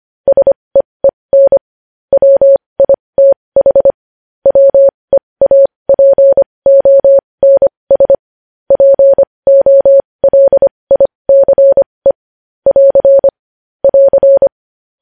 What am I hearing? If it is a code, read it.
Morse code, heard as a keyed tone, about 25 wpm, spaced as for 20 wpm, 550 Hz. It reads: SEEN WITH WEAPONS POLICE <AR> <AR>
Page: news headlines in morse code at 20 wpm